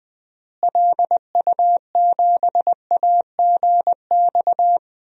Morse code, heard as a keyed tone, about 21 wpm, 700 Hz. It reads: LU7AGX
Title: DXCC-100 Call Signs - 6 Characters - Repeated 3x - 20wpm